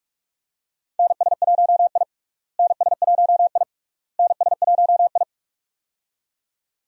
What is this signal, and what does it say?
Morse code, heard as a keyed tone, about 46 wpm, 700 Hz. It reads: NS1I NS1I NS1I
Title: USA Call Signs - 4 Characters - Repeated 3x - 45wpm